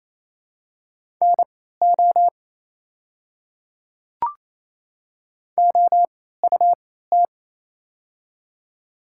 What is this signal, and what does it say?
Morse code, heard as a keyed tone, about 28 wpm, spaced as for 16 wpm, 700 Hz. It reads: NO OUT
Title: 3x Character Spacing - Top 100 Words - 28wpm